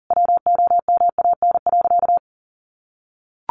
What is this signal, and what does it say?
Morse code, heard as a keyed tone, about 40 wpm, 700 Hz. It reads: WOMAN.
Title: Sentences from Top 1400 Words 40wpm